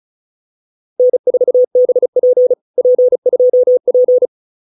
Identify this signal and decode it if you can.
Morse code, heard as a keyed tone, about 34 wpm, 530 Hz. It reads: N4BP P2P